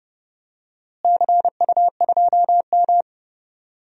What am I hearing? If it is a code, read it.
Morse code, heard as a keyed tone, about 30 wpm, 700 Hz. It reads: CU2M
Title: DXCC-100 Call Signs - 4 Characters - Repeated 2x - 30wpm